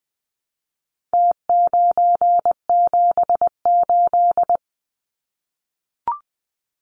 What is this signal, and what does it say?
Morse code, heard as a keyed tone, about 20 wpm, 700 Hz. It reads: T978
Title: Numbers - Contest Style - 4-Digit 20wpm